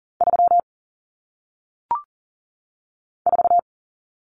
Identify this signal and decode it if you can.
Morse code, heard as a keyed tone, about 40 wpm, 700 Hz. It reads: 3 4
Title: Common Cut Numbers - 1-Digit 40wpm